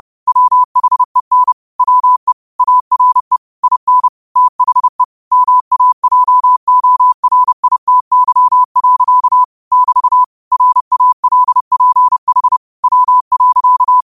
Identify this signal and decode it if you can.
Morse code, heard as a keyed tone, about 30 wpm, 1000 Hz. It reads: WHEN WE ARE IN THE MAJORITY. <BT> RALPH W.